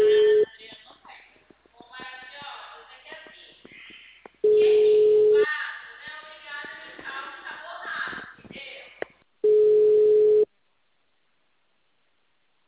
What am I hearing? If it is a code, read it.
Ringback tone (Europe, 425 Hz), with speech.
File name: out- -2044- - - .73379